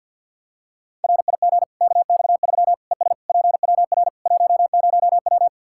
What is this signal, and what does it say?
Morse code, heard as a keyed tone, about 51 wpm, 700 Hz. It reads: RIG KX3 ES PWR 10W
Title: Practice QSO - Exchange 2 - Info - 50wpm